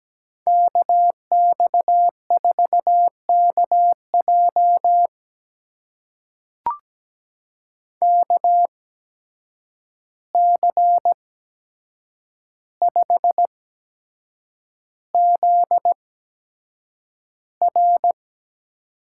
Morse code, heard as a keyed tone, about 17 wpm, spaced as for 7 wpm, 700 Hz. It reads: KX4KJ KC5ZR
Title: USA Call Signs - 5 Characters - 8x Character Spacing - 17wpm